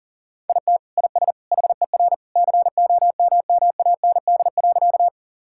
Morse code, heard as a keyed tone, about 37 wpm, 700 Hz. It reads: IT IS HER COMMAND.
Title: Sentences from Top 1600 Words 40wpm